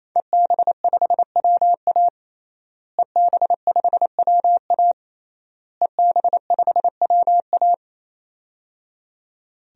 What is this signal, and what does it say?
Morse code, heard as a keyed tone, about 28 wpm, 700 Hz. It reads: EB5WA EB5WA EB5WA
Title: DXCC-100 Call Signs - 5 Characters - Repeated 3x - 28wpm